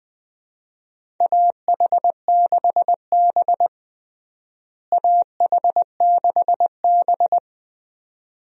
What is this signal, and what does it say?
Morse code, heard as a keyed tone, about 20 wpm, 700 Hz. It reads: AH6B AH6B
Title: USA Call Signs - 4 Characters - Repeated 2x - 20wpm